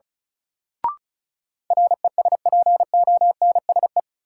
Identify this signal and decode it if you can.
Morse code, heard as a keyed tone, about 34 wpm, 700 Hz. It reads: RESPONSE
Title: Sentences from Top 1400 Words - New Word Drill Alt - 35wpm